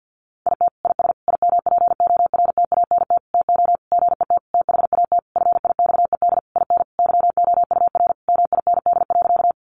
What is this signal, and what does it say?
Morse code, heard as a keyed tone, about 50 wpm, 700 Hz. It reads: IT IS IMPORTANT TO GET THAT FIXED IN YOUR MIND.
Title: Sentences from Top 1100 Words 50wpm